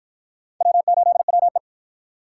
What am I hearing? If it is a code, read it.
Morse code, heard as a keyed tone, about 53 wpm, 700 Hz. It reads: W8WE